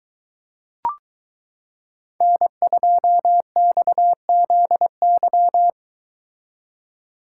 Morse code, heard as a keyed tone, about 22 wpm, 700 Hz. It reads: N2XZY